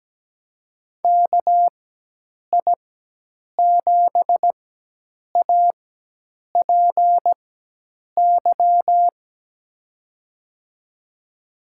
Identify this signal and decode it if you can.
Morse code, heard as a keyed tone, about 17 wpm, spaced as for 8 wpm, 700 Hz. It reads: KI7APY
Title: USA Call Signs - 6 Characters - 4x Character Spacing - 17wpm